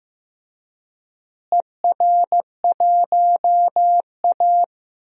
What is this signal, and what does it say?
Morse code, heard as a keyed tone, about 15 wpm, 700 Hz. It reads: ER1A
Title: DXCC-100 Call Signs - 4 Characters - Repeated 3x - 15wpm